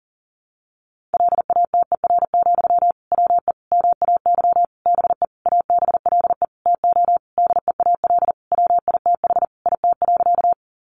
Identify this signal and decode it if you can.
Morse code, heard as a keyed tone, about 40 wpm, 700 Hz. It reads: LATER, WE MAY BE ABLE TO DEAL WITH IT.
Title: Sentences from Top 800 Words 40wpm